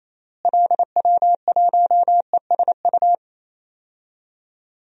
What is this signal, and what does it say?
Morse code, heard as a keyed tone, about 28 wpm, 700 Hz. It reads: LW1ESU
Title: DXCC-100 Call Signs - 6 Characters - Repeated 3x - 28wpm